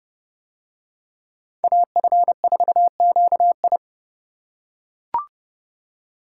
Morse code, heard as a keyed tone, about 29 wpm, 700 Hz. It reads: AF4QI